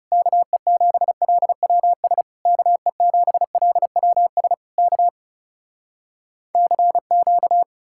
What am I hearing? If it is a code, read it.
Morse code, heard as a keyed tone, about 40 wpm, 700 Hz. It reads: KE7LWS KE7LWS K CQ